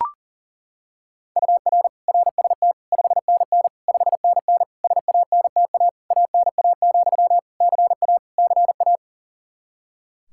Morse code, heard as a keyed tone, about 37 wpm, 700 Hz. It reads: UR RST 5NN 5NN SANTA ANA, CA CA
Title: Practice QSO - Exchange 1 - Beginning - 40wpm